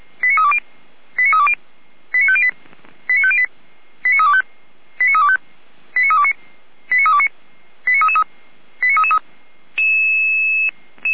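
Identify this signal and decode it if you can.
Five-tone selective call (ZVEI), heard as quick runs of five tones, tones of about 70 ms. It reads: 89320 89320 89508 89508 89325 89325 89329 89329 89333 89333
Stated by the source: ZVEI five-tone call 89202 received with an SDR.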